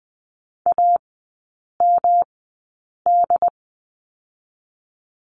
Morse code, heard as a keyed tone, about 20 wpm, spaced as for 8 wpm, 700 Hz. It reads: AMD
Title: 121 Lesson 20e - Number 3 - Three Characters 20wpm